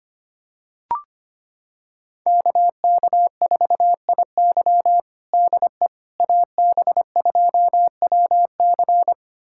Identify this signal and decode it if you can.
Morse code, heard as a keyed tone, about 25 wpm, 700 Hz. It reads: KK4IY DE AB2WC
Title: USA Callsign DE Callsign - 5 Characters - 25wpm